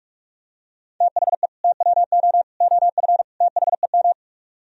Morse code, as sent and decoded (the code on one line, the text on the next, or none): - .... / - .-- --- / --- ..-. / - .... --
THE TWO OF THEM